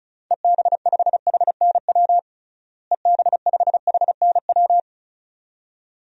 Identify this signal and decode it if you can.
Morse code, heard as a keyed tone, about 34 wpm, 700 Hz. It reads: EB5HNW EB5HNW
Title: DXCC-100 Call Signs - 6 Characters - Repeated 2x - 35wpm